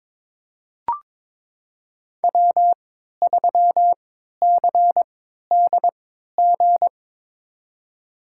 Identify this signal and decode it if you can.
Morse code, heard as a keyed tone, about 22 wpm, spaced as for 13 wpm, 700 Hz. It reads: W3CDG
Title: USA Call Signs - 3x Character Spacing - 22wpm